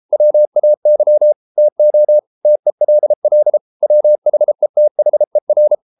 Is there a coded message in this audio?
Yes: Morse code from 0.1 to 5.8 s, heard as a keyed tone, about 33 wpm, 600 Hz. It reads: WAY TO TELL WHETHER